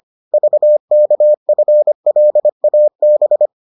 Morse code, heard as a keyed tone, about 25 wpm, 600 Hz. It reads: VKFLAB